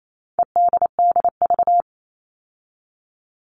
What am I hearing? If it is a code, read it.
Morse code, heard as a keyed tone, about 28 wpm, 700 Hz. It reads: EDDV
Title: All Cut Numbers - 4-Digit 28wpm